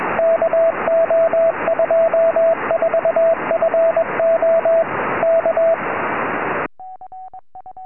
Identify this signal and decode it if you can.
Morse code, heard as a keyed tone, about 21 wpm, 660 Hz, with radio noise. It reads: KO24FO K
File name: tropo / ly2wr 2m